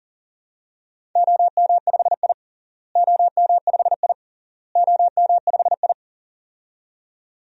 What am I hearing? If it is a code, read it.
Morse code, heard as a keyed tone, about 40 wpm, 700 Hz. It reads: OM5I OM5I OM5I
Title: DXCC-100 Call Signs - 4 Characters - Repeated 3x - 40wpm